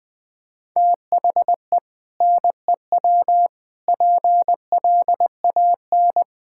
Morse code, heard as a keyed tone, about 20 wpm, 700 Hz. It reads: THE NEW PLAN
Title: Sentences with 2-Word Phrases - Spoken Beginning - 20wpm